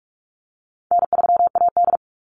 Morse code, heard as a keyed tone, about 44 wpm, 700 Hz. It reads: N3AD
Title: Call Signs Repeated 2x - 45wpm